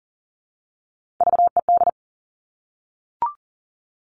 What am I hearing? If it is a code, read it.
Morse code, heard as a keyed tone, about 40 wpm, 700 Hz. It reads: VED
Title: All Cut Numbers - 3-Digit 40wpm